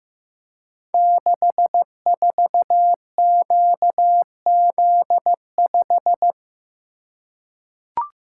Morse code, heard as a keyed tone, about 15 wpm, 700 Hz. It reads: 64QZ5
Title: Code Groups - 5 Letter:Numbers - 15wpm